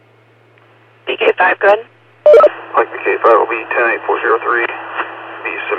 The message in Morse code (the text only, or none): none